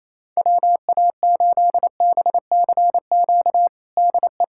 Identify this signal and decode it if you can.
Morse code, heard as a keyed tone, about 28 wpm, 700 Hz. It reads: WA8BCQ DE
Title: USA Callsign DE Callsign - 6 Characters - 28wpm